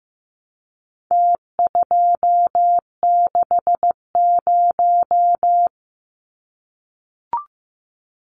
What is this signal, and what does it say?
Morse code, heard as a keyed tone, about 15 wpm, 700 Hz. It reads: T260
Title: Numbers - Contest Style - 4-Digit 15wpm